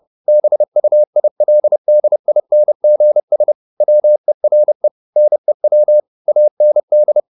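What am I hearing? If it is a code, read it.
Morse code, heard as a keyed tone, about 30 wpm, 600 Hz. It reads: BUILDINGS WERE NEW AND
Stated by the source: Morse code at 29 wpm, sending Ken Follett's novel Never in English, from CW med Gnister.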